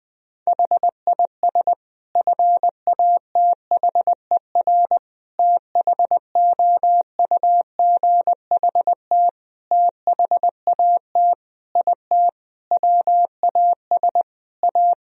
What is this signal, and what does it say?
Morse code, heard as a keyed tone, about 20 wpm, 700 Hz. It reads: HIS FATHER THOUGHT THAT IT WAS A
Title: Sentences from Top 1000 Words - Review Part 4 - 20wpm